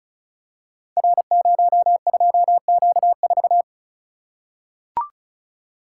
Morse code, heard as a keyed tone, about 34 wpm, 700 Hz. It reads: R02Q4
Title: Code Groups - 5 Letter:Numbers - 35wpm